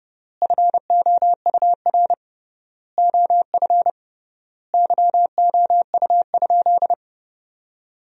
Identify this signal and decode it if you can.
Morse code, heard as a keyed tone, about 30 wpm, 700 Hz. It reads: FOUR OF YOU?
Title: 3x Word-Spacing - Sentences from Top 300 Words - 30wpm